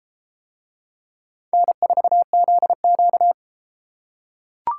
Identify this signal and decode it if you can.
Morse code, heard as a keyed tone, about 32 wpm, 700 Hz. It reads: N4ZQ